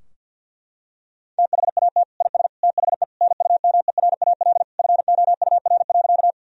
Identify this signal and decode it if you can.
Morse code, heard as a keyed tone, about 51 wpm, 700 Hz. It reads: THAT IS THE NUMERAL FOUR.
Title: Sentences from Top 1000 Words - Review Part 2 - 50wpm